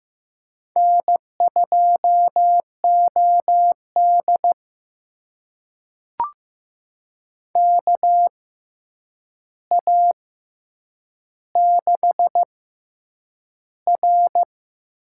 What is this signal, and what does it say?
Morse code, heard as a keyed tone, about 15 wpm, spaced as for 8 wpm, 700 Hz. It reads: N2OD KA6R
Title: USA Call Signs - 6x Character Spacing - 15wpm